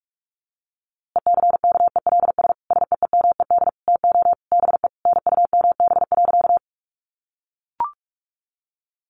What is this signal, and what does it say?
Morse code, heard as a keyed tone, about 44 wpm, 700 Hz. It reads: ECKELS SEEMED TO BE NUMB.
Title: A Sound Of Thunder 45wpm